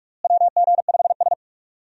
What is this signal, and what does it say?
Morse code, heard as a keyed tone, about 46 wpm, 700 Hz. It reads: WG5S
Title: USA Call Signs - 4 Characters - Repeated 2x - 45wpm